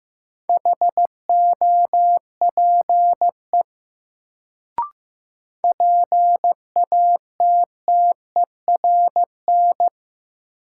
Morse code, heard as a keyed tone, about 15 wpm, 700 Hz. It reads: HOPE PATTERN
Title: Sentences from Top 1000 Words - New Word Drill - 15wpm